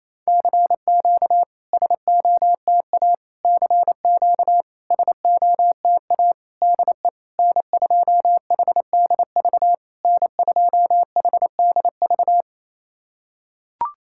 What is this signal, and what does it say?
Morse code, heard as a keyed tone, about 28 wpm, 700 Hz. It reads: CQ SOTA CQ SOTA DE N2HDV N2HDV
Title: SOTA QSO Practice 28wpm